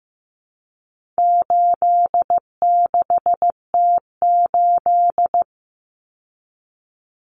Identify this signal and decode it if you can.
Morse code, heard as a keyed tone, about 15 wpm, 700 Hz. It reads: 86T8
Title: Common Cut Numbers - 4-Digit 15wpm